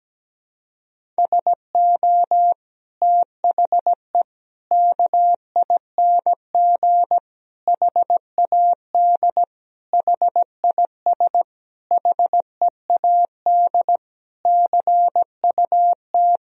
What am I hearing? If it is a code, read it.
Morse code, heard as a keyed tone, about 17 wpm, 700 Hz. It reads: SO THE KING HAD HIS HEAD CUT
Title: Sentences from Top 1000 Words - Review Part 1 - 17wpm